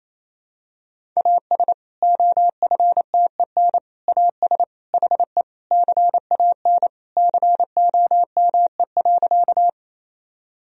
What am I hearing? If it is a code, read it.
Morse code, heard as a keyed tone, about 28 wpm, 700 Hz. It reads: AS OFTEN AS HE CAN COME.
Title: Sentences from Top 1000 Words - Review Part 1 - 28wpm